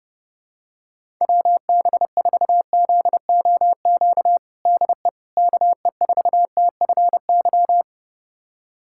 Morse code, heard as a keyed tone, about 30 wpm, 700 Hz. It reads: WB4ZOQ DE KE4TFY
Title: USA Callsign DE Callsign 30wpm